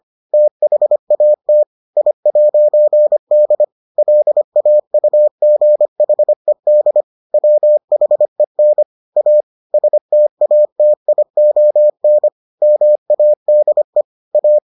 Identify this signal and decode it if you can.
Morse code, heard as a keyed tone, about 25 wpm, 600 Hz. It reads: THAT I'D LAUGHED WHEN A STATION MADE A